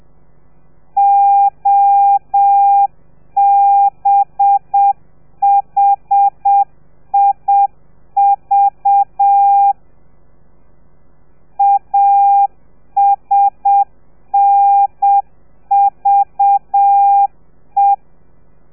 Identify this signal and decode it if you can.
Morse code, heard as a keyed tone, about 7 wpm, 800 Hz. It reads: OBHIV ASNVE
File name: telegrafia / i2tfj / Lez4 30